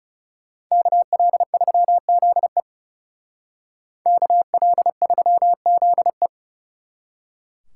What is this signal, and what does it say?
Morse code, heard as a keyed tone, about 40 wpm, 700 Hz. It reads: KL3ZE KL3ZE